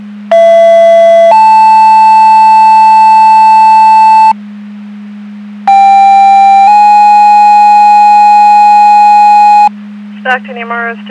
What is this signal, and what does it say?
Two-tone page, heard as a one-second tone, then a three-second tone, about 670 then 880 Hz, about 790 then 830 Hz, with speech. It is heard from 0.3 to 9.7 s.